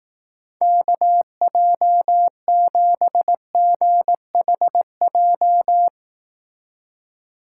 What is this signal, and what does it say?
Morse code, heard as a keyed tone, about 19 wpm, 700 Hz. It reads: KJ7GHJ
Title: Speed Racing - USA 2x3 Call Signs 15wpm